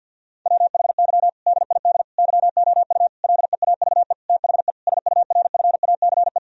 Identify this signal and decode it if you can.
Morse code, heard as a keyed tone, about 50 wpm, 700 Hz. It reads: WHY DID YOU LEAVE THE SURFACE
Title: Sentences from Top 1000 Words - Review Part 3 - 50wpm